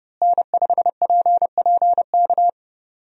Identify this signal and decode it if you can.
Morse code, heard as a keyed tone, about 30 wpm, 700 Hz. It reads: N5PPK
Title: USA Call Signs - 5 Characters - Repeated 3x - 30wpm